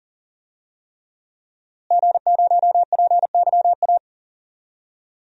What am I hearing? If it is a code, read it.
Morse code, heard as a keyed tone, about 40 wpm, 700 Hz. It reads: G0PYA
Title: DXCC-100 Call Signs - 5 Characters - Repeated 3x - 40wpm